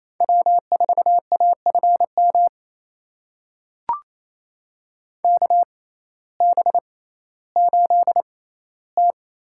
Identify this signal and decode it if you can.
Morse code, heard as a keyed tone, about 28 wpm, spaced as for 15 wpm, 700 Hz. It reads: W4AFM KB8T
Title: USA Call Signs - 6x Character Spacing - 28wpm